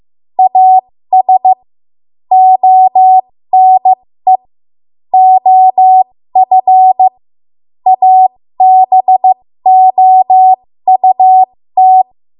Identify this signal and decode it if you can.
Morse code, heard as a keyed tone, about 15 wpm, spaced as for 13 wpm, 750 Hz. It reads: AS ONE OF ABOUT
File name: Morse / 13WPM